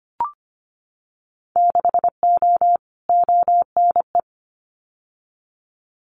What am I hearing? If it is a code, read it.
Morse code, heard as a keyed tone, about 25 wpm, 700 Hz. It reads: 6O ONE
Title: Winter Field Day Exchanges Encore 25wpm